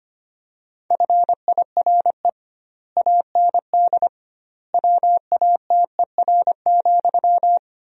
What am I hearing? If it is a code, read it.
Morse code, heard as a keyed tone, about 25 wpm, 700 Hz. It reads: FIRE AND WATER,
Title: 2x Word-Spacing - Sentences from Top 400 Words - 25wpm